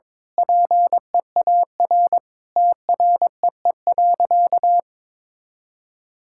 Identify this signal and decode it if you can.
Morse code, heard as a keyed tone, about 22 wpm, 700 Hz. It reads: PEAR TREE.
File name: Sentences with 9-Word Phrases - 22wpm